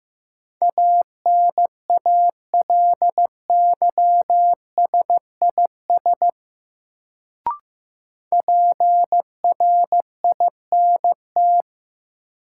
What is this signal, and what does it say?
Morse code, heard as a keyed tone, about 15 wpm, 700 Hz. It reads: ANALYSIS PRINT
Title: Sentences from Top 1400 Words - New Word Drill Alt - 15wpm